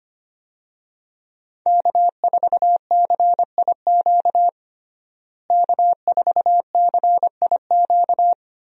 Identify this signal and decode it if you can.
Morse code, heard as a keyed tone, about 25 wpm, 700 Hz. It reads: K4CIQ K4CIQ
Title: USA Call Signs - 5 Characters - Repeated 3x - 25wpm